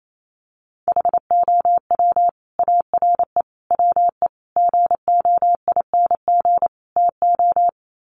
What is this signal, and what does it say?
Morse code, heard as a keyed tone, about 28 wpm, 700 Hz. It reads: HOW ARE WE GOING TO
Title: Sets of 5 Words Encore 28wpm